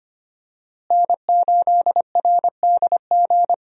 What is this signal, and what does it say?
Morse code, heard as a keyed tone, about 25 wpm, 700 Hz. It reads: N8RDG
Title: USA Call Signs - 5 Characters - Repeated 2x - 25wpm